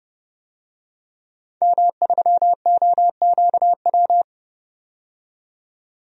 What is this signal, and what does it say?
Morse code, heard as a keyed tone, about 32 wpm, 700 Hz. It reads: M3OQW
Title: DXCC-100 Call Signs - 5 Characters - 30wpm